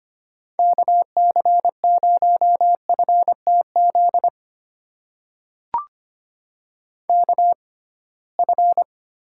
Morse code, heard as a keyed tone, about 25 wpm, spaced as for 19 wpm, 700 Hz. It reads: KC0FTZ KF
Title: USA Call Signs - 6x Character Spacing - 25wpm